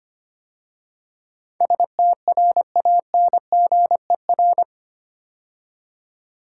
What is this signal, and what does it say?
Morse code, heard as a keyed tone, about 25 wpm, 700 Hz. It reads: STRANGER
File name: 8 Letter Words - First 3 Letters Given - 25wpm